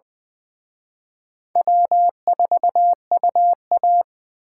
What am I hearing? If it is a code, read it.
Morse code, heard as a keyed tone, about 24 wpm, 700 Hz. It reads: W4UA
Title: Speed Racing - USA 1x2 Call Signs 20wpm